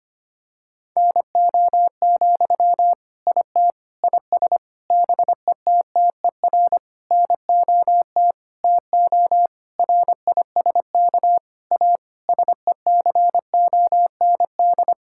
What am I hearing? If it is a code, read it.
Morse code, heard as a keyed tone, about 25 wpm, 700 Hz. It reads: NO, IT IS BETTER NOT TO RISK A SECOND
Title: Sentences from Top 1600 Words 25wpm